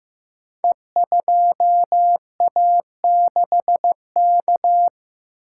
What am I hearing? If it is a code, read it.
Morse code, heard as a keyed tone, about 15 wpm, 700 Hz. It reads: E2A6K
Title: Code Groups - 5 Letter:Numbers - 15wpm